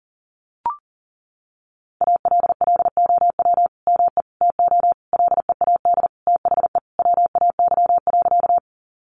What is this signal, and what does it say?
Morse code, heard as a keyed tone, about 40 wpm, 700 Hz. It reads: ALLOW ME TO LEAD THE WAY.
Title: Sentences from Top 700 Words 40wpm